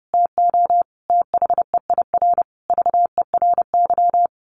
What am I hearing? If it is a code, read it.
Morse code, heard as a keyed tone, about 30 wpm, 700 Hz. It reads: TO THEIR VERY
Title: A Sound Of Thunder 30wpm